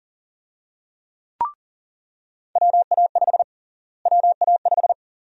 Morse code, heard as a keyed tone, about 37 wpm, 700 Hz. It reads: WA5 WA5
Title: USA Call Sign Prefixes - Repeated 2x - 40wpm